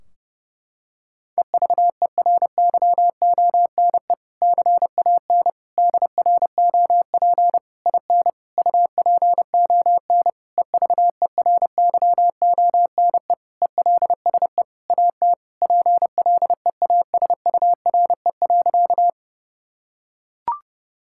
Morse code, heard as a keyed tone, about 30 wpm, 700 Hz. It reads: EVERYONE CAN DROP IN UPON EVERYONE ELSE AT PLEASURE.
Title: Sentences from Top 1600 Words 30wpm